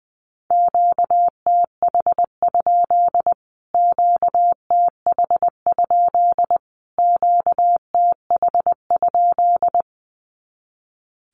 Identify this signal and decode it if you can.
Morse code, heard as a keyed tone, about 19 wpm, 700 Hz. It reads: QTH? QTH? QTH?
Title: Q-Codes Repeated 3x 20wpm